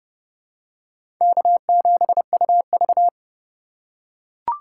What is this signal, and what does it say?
Morse code, heard as a keyed tone, about 31 wpm, 700 Hz. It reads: K7UV
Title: USA Call Signs - 4 Characters - 3x Character Spacing - 30wpm